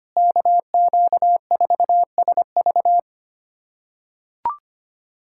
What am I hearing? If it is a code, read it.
Morse code, heard as a keyed tone, about 25 wpm, 700 Hz. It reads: KQ4SV
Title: USA Callsign DE Callsign - 5 Characters - 25wpm